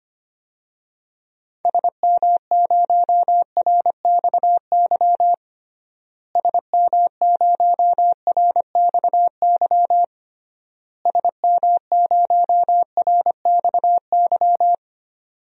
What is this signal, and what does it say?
Morse code, heard as a keyed tone, about 25 wpm, 700 Hz. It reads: SM0RXY SM0RXY SM0RXY
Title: DXCC-100 Call Signs - 6 Characters - Repeated 3x - 25wpm